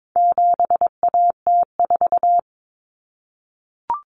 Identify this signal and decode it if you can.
Morse code, heard as a keyed tone, about 22 wpm, 700 Hz. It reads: 7AT4
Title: Common Cut Numbers - 4-Digit 22wpm